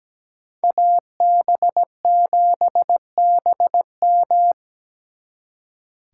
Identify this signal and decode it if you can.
Morse code, heard as a keyed tone, about 20 wpm, 700 Hz. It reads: AB7BM
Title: Speed Racing - USA 2x2 Call Signs 17wpm